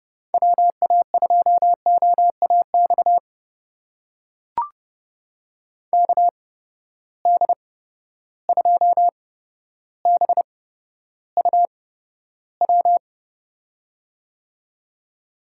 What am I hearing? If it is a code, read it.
Morse code, heard as a keyed tone, about 30 wpm, spaced as for 13 wpm, 700 Hz. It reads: WA2OAX KD2BUW
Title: USA Call Signs - 6 Characters - 8x Character Spacing - 30wpm